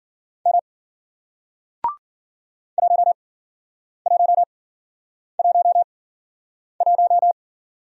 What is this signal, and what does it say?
Morse code, heard as a keyed tone, about 54 wpm, spaced as for 9 wpm, 700 Hz. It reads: M 1111